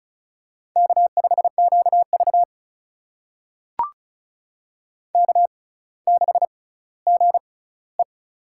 Morse code, heard as a keyed tone, about 36 wpm, spaced as for 18 wpm, 700 Hz. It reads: K5QV K6GE
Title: USA Call Signs - 4 Characters - 6x Character Spacing - 35wpm